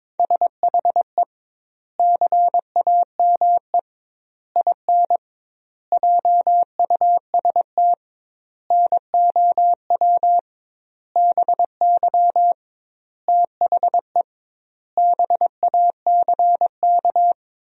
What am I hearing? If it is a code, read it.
Morse code, heard as a keyed tone, about 22 wpm, 700 Hz. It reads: SHE CAME IN JUST NOW BY THE BACK
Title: 2x Word-Spacing - Sentences from Top 300 Words - 22wpm